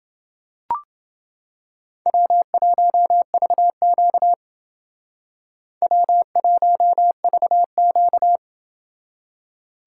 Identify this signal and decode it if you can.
Morse code, heard as a keyed tone, about 28 wpm, 690 Hz. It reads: W1VQ W1VQ